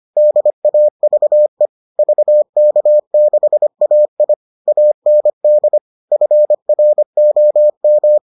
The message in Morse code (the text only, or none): DAVE VK6AI AND FROM